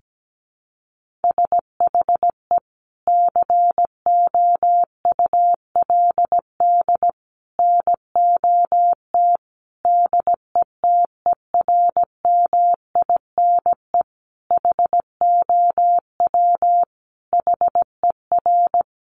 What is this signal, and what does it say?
Morse code, heard as a keyed tone, about 17 wpm, 700 Hz. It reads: SHE COULD NOT DETERMINE HOW HER
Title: Sentences from Top 700 Words 17wpm